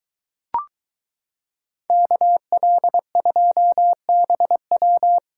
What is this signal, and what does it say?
Morse code, heard as a keyed tone, about 23 wpm, 700 Hz. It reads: KL2BW